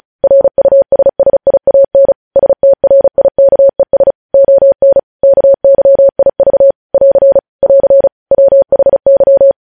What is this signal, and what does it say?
Morse code, heard as a keyed tone, about 36 wpm, 550 Hz. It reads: RUSSIAN STRIKES ON KYIV <AR> <AR> WHY
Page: News Headlines in Morse Code at 35 WPM